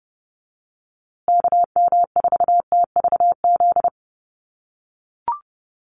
Morse code, heard as a keyed tone, about 30 wpm, 700 Hz. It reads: KM4TVZ
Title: USA Call Signs Repeated 3x 30wpm